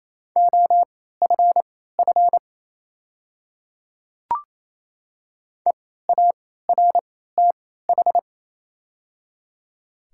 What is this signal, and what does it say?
Morse code, heard as a keyed tone, about 28 wpm, spaced as for 16 wpm, 690 Hz. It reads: OFF EARTH
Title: 3x Character Spacing - Top 500 Words - 28wpm